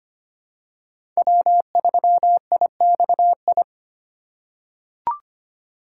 Morse code, heard as a keyed tone, about 25 wpm, 700 Hz. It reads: W3IXI
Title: USA Call Signs - 5 Characters - 4x Character Spacing - 25wpm